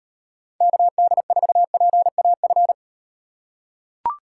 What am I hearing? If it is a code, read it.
Morse code, heard as a keyed tone, about 38 wpm, 700 Hz. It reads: KD4PAF